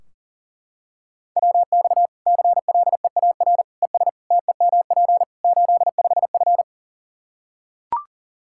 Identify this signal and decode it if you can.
Morse code, heard as a keyed tone, about 40 wpm, 700 Hz. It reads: WX CLEAR ES TEMP 85F
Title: Practice QSO - Exchange 2 - Info - 40wpm